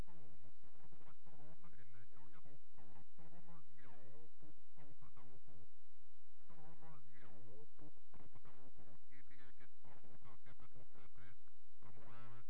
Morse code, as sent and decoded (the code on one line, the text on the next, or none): none